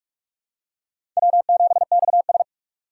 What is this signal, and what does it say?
Morse code, heard as a keyed tone, about 46 wpm, 700 Hz. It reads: W7XS
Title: USA Call Signs - 4 Characters - 2x Character Spacing - 45wpm